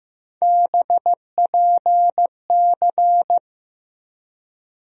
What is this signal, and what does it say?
Morse code, heard as a keyed tone, about 15 wpm, 700 Hz. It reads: BPC
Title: USA Call Sign Postfixes - 3 Letters 15wpm